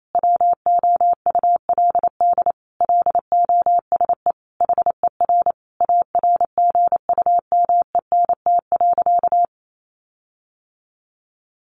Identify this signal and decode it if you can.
Morse code, heard as a keyed tone, about 28 wpm, 700 Hz. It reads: WOULD LOSE HER ARGUMENT.
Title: Sentences from Top 1400 Words 28wpm